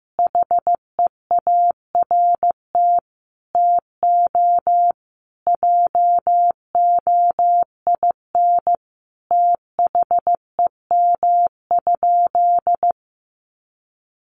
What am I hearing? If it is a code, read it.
Morse code, heard as a keyed tone, about 15 wpm, 700 Hz. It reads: HEART TO JOIN THEM?
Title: Sentences from Top 1400 Words 15wpm